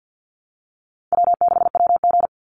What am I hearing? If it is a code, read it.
Morse code, heard as a keyed tone, about 48 wpm, 700 Hz. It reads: W6WG
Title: Easy Call Signs Repeated 3x - 50wpm